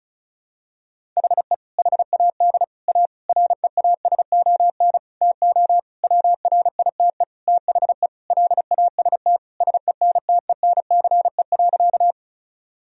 Morse code, heard as a keyed tone, about 36 wpm, 700 Hz. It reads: HE HAD A REASON TO WRITE THE LAST SENTENCE.
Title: Sentences from Top 1000 Words - Review Part 1 - 35wpm